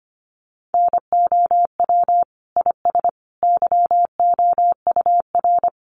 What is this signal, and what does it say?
Morse code, heard as a keyed tone, about 25 wpm, 700 Hz. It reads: NOW IS YOUR
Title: Sentences from Top 100 Words 25wpm